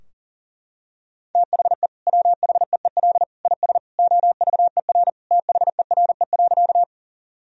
Morse code, heard as a keyed tone, about 40 wpm, 700 Hz. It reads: THE WHEEL IS OVER THERE.
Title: Sentences from Top 1000 Words - Review Part 3 - 40wpm